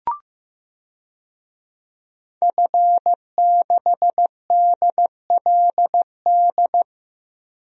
Morse code, heard as a keyed tone, about 15 wpm, 700 Hz. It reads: F6DLD
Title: DXCC-100 Call Signs - 5 Characters - Repeated 3x - 15wpm